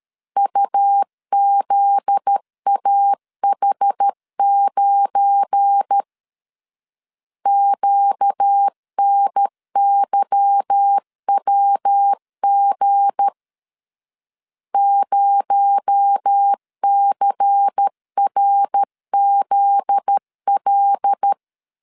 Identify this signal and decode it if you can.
Morse code, heard as a keyed tone, about 13 wpm, 800 Hz. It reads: UZAH9 QNYWG 0CRZL
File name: LEZIONI / 8 YJ45 / 8 Y 12wpm 2228